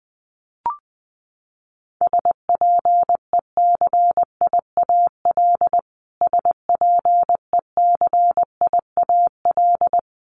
Morse code, heard as a keyed tone, about 20 wpm, 700 Hz. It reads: SPECIAL SPECIAL
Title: Top 500 Words Repeated 20wpm